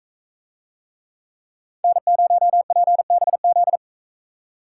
Morse code, heard as a keyed tone, about 42 wpm, 700 Hz. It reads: N0PBZ